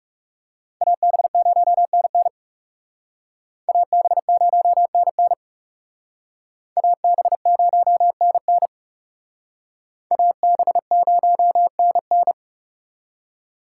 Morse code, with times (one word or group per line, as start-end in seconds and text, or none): 0.8-2.3 AB0NN
3.7-5.3 AB0NN
6.8-8.7 AB0NN
10.1-12.3 AB0NN